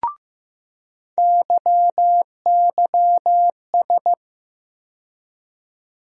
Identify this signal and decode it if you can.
Morse code, heard as a keyed tone, about 15 wpm, 700 Hz. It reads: YYS
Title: USA Call Sign Postfixes - 3 Letters 15wpm